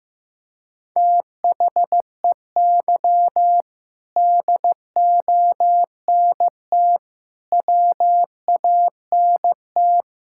Morse code, heard as a keyed tone, about 15 wpm, 700 Hz. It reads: THEY DONT WANT
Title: Sentences from Top 1000 Words - Review Part 5 - 15wpm